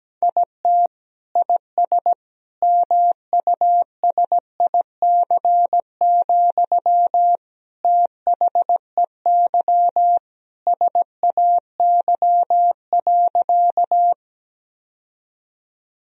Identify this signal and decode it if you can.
Morse code, heard as a keyed tone, about 17 wpm, 700 Hz. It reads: IT IS MUSIC, THEY SAY.
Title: Sentences from Top 1000 Words - Review Part 1 - 17wpm